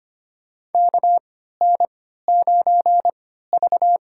Morse code, heard as a keyed tone, about 25 wpm, spaced as for 14 wpm, 700 Hz. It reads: KN9V
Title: USA Call Signs - 4 Characters - 3x Character Spacing - 25wpm